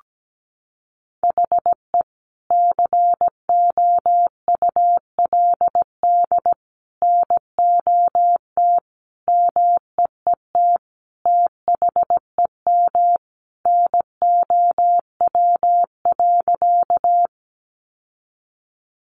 Morse code, heard as a keyed tone, about 17 wpm, 700 Hz. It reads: HE COULD NOT MEET THEM NOW.
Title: Sentences from Top 1100 Words 17wpm